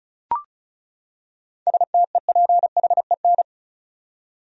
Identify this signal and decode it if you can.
Morse code, heard as a keyed tone, about 36 wpm, 700 Hz. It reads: STEPHEN
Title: USA Common Names - 35wpm